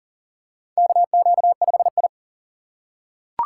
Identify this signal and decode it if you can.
Morse code, heard as a keyed tone, about 40 wpm, 700 Hz. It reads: KQ5I
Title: USA Call Signs - 3x Character Spacing - 40wpm